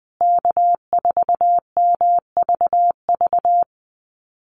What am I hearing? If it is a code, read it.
Morse code, heard as a keyed tone, about 20 wpm, 700 Hz. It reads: K4MVV
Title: USA Call Signs Repeated 3x 20wpm